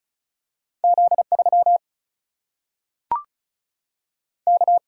Morse code, heard as a keyed tone, about 36 wpm, 700 Hz. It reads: Z 3 K
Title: Single-Letter Single-Number 35wpm